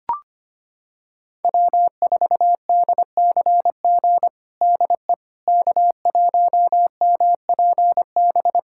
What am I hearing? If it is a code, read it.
Morse code, heard as a keyed tone, about 25 wpm, 700 Hz. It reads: W4DCG DE K1MPB
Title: USA Callsign DE Callsign - 5 Characters - 25wpm